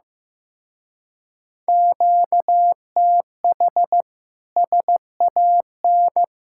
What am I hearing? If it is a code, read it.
Morse code, heard as a keyed tone, about 15 wpm, 700 Hz. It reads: QTH SAN
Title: Speed-Racing Common QSO Phrases Encore - 15wpm